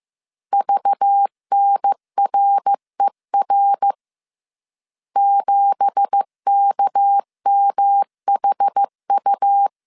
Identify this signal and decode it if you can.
Morse code, heard as a keyed tone, about 15 wpm, 800 Hz. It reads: VNRER 7KMHU
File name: CW / DET / 1 9 1500C S C 14wpm 3245